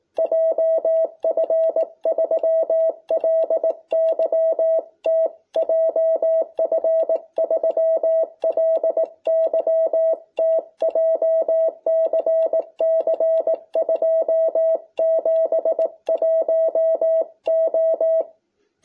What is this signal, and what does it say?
Morse code, heard as a keyed tone, about 18 wpm, 660 Hz. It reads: JF3LYT JF3LYT JCC 271O